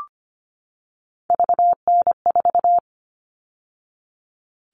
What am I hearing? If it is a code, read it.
Morse code, heard as a keyed tone, about 25 wpm, 700 Hz. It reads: VN4
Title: All Cut Numbers - 3-Digit 25wpm